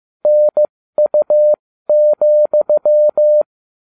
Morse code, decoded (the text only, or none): NU,